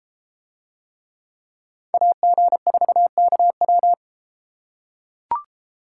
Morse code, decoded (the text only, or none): AG4KW